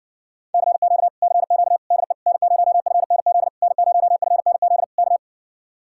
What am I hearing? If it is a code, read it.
Morse code, heard as a keyed tone, about 52 wpm, 700 Hz. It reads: CQ CQ DE N0FNZ N0FNZ K